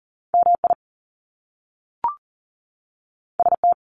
Morse code, heard as a keyed tone, about 38 wpm, 700 Hz. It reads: MI ST
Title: Top Two Letter Combinations in Words 40wpm